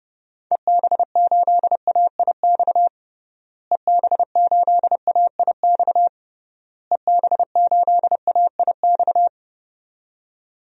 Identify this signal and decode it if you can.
Morse code, heard as a keyed tone, about 30 wpm, 700 Hz. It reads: EB8AIX EB8AIX EB8AIX
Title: DXCC-100 Call Signs - 6 Characters - Repeated 3x - 30wpm